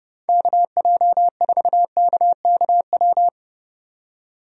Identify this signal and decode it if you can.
Morse code, heard as a keyed tone, about 30 wpm, 700 Hz. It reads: KJ4KKW
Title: USA Call Signs - 6 Characters - 30wpm